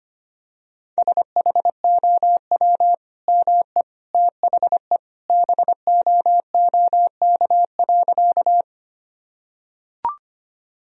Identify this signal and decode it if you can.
Morse code, heard as a keyed tone, about 25 wpm, 700 Hz. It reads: SHOW ME THE BOOK.
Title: Sentences from Top 1000 Words - Review Part 1 - 25wpm